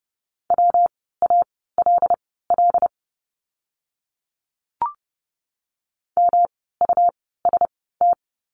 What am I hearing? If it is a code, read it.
Morse code, heard as a keyed tone, about 30 wpm, spaced as for 17 wpm, 700 Hz. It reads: WALL MUST
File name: course / 30:10wpm / 115 Lesson 19f - Letter w - Words 30wpm